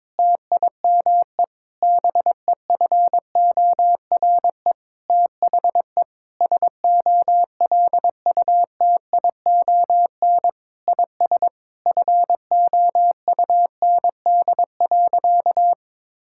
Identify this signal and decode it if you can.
Morse code, heard as a keyed tone, about 22 wpm, 700 Hz. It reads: TIME BEFORE THE SOLUTION IS FOUND.
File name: Sentences with 9-Word Phrases - 22wpm